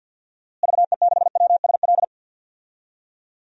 Morse code, heard as a keyed tone, about 47 wpm, 700 Hz. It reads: VE6WSL
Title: DXCC-100 Call Signs - 6 Characters - Repeated 2x - 50wpm